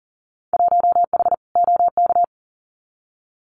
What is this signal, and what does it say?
Morse code, heard as a keyed tone, about 40 wpm, 700 Hz. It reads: 1H OK
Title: Winter Field Day Exchanges 40wpm